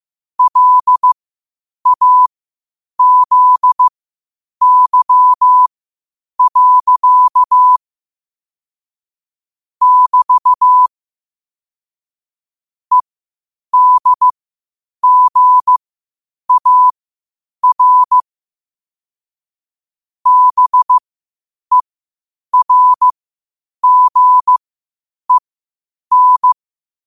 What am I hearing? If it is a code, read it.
Morse code, heard as a keyed tone, about 15 wpm, spaced as for 8 wpm, 1000 Hz. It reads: LAZY. <BT> EDGAR BERGEN
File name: qotdfiles / 8wpm